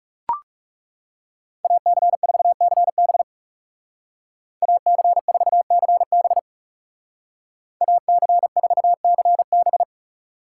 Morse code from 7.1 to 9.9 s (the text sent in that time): AC4CB